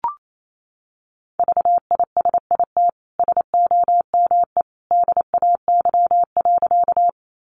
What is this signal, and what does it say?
Morse code, heard as a keyed tone, about 28 wpm, 700 Hz. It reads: VISIT SOME DAY.
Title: Sentences from Top 900 Words 28wpm